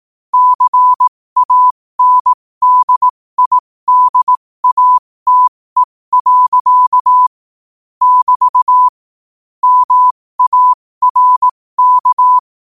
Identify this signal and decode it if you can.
Morse code, heard as a keyed tone, about 18 wpm, spaced as for 15 wpm, 1000 Hz. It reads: CANDIDATE. <BT> MARK
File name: qotdfiles / 15wpm